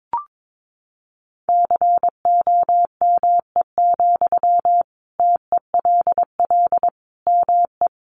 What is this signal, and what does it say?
Morse code, heard as a keyed tone, about 22 wpm, 700 Hz. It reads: COME, TELL ME
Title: Sentences from Top 1200 Words 22wpm